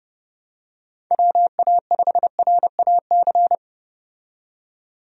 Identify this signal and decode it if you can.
Morse code, heard as a keyed tone, about 32 wpm, 700 Hz. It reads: WA5RAC
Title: USA Call Signs - 6 Characters - 30wpm